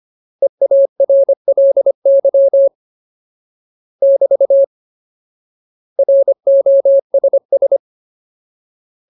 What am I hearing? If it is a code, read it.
Morse code, heard as a keyed tone, about 25 wpm, 550 Hz. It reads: EARLY <BT> ROSS
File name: Ultimate Word List / 25WPM Ultimate